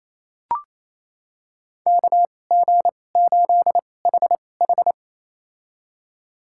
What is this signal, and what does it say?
Morse code, heard as a keyed tone, about 28 wpm, spaced as for 20 wpm, 700 Hz. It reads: KG8HH
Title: USA Call Signs - 5 Characters - 2x Character Spacing - 28wpm